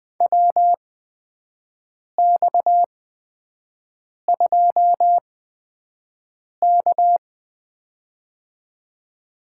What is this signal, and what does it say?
Morse code, heard as a keyed tone, about 21 wpm, spaced as for 5 wpm, 700 Hz. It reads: WX2K
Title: USA Call Signs - 4 Characters - 8x Character Spacing - 20wpm